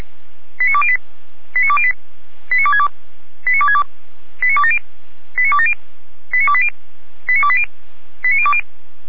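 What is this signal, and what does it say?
Five-tone selective call (ZVEI), heard as quick runs of five tones, tones of about 70 ms. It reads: 89208 89208 89262 89262 89270 89270 89280 89280 89020
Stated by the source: ZVEI five-tone selective call, address 89340, received on an SDR.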